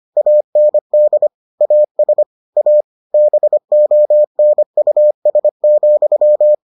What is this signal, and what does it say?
Morse code, heard as a keyed tone, about 25 wpm, 600 Hz. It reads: AND AS A BONUS,